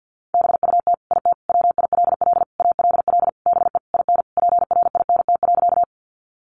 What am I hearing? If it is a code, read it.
Morse code, heard as a keyed tone, about 50 wpm, 700 Hz. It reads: BUT IT WILL ALL BE IN PRINT.
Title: Sentences from Top 1400 Words 50wpm